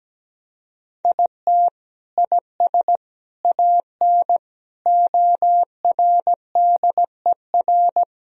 Morse code, heard as a keyed tone, about 17 wpm, 700 Hz. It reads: IT IS AN ORDER
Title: Sentences from Top 1000 Words - Review Part 2 - 17wpm